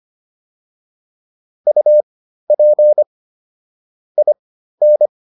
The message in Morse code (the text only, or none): UP IN